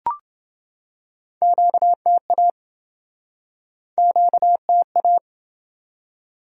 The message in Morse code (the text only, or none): QTA QTA